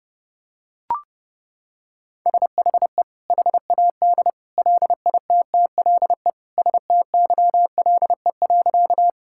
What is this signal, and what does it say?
Morse code, heard as a keyed tone, about 32 wpm, 700 Hz. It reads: SHE HAD LITTLE STYLE.
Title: Sentences from Top 1000 Words - Review Part 5 - 30wpm